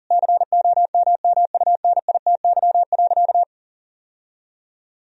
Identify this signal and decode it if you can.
Morse code, heard as a keyed tone, about 40 wpm, 700 Hz. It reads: COMMUNITY.
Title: Sentences from Top 1000 Words - Review Part 4 - 40wpm